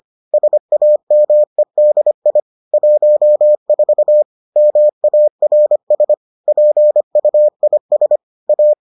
Morse code, heard as a keyed tone, about 25 wpm, 600 Hz. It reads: SAMEDI 14 MARS PUIS A
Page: bulletin UFT sous forme texte et en CW